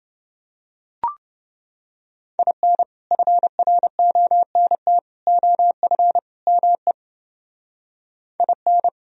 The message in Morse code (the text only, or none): IN FRONT OF ME IN